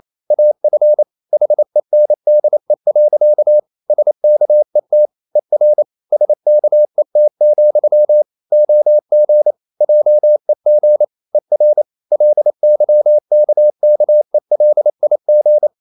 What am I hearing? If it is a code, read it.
Morse code, heard as a keyed tone, about 28 wpm, 600 Hz. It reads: AF HENDE. SKET ER SKET, OG JEG ER LYKKELIG
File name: Ferie QRQ 28wpm